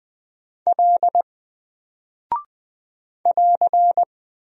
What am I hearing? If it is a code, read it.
Morse code, heard as a keyed tone, about 20 wpm, 700 Hz. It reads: L <AR>
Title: Single Character (Letter:Number:Punctuation:Prosign) 20wpm